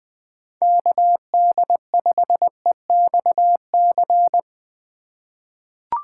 Morse code, heard as a keyed tone, about 20 wpm, 700 Hz. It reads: KD5EXC
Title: USA Call Signs - 6 Characters - 4x Character Spacing - 20wpm